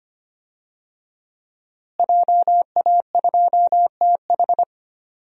Morse code, heard as a keyed tone, about 25 wpm, 700 Hz. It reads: JA2TH
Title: DXCC-100 Call Signs - 5 Characters - Repeated 3x - 25wpm